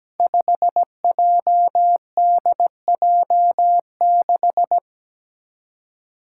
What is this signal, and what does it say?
Morse code, heard as a keyed tone, about 17 wpm, 700 Hz. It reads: 5JDJ6
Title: Five Challenging Letter-Number Combinations 17wpm